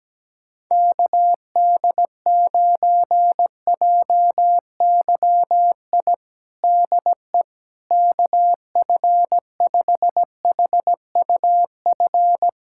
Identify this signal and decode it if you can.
Morse code, heard as a keyed tone, about 17 wpm, 700 Hz. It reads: KD9JYI DE KF5HUF
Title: USA Callsign DE Callsign 17wpm